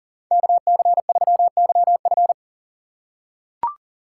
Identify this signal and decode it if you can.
Morse code, heard as a keyed tone, about 40 wpm, 700 Hz. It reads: KC3YF
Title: USA Call Signs - 5 Characters - 6x Character Spacing - 40wpm